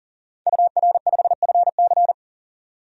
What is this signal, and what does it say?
Morse code, heard as a keyed tone, about 39 wpm, 700 Hz. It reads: UR5FC
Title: DXCC-100 Call Signs - 5 Characters - 40wpm